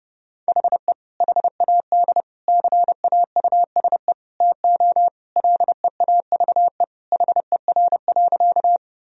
Morse code, heard as a keyed tone, about 30 wpm, 700 Hz. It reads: HE HAD CAUSE TO LEAVE HER.
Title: Sentences from Top 1000 Words - Review Part 1 - 30wpm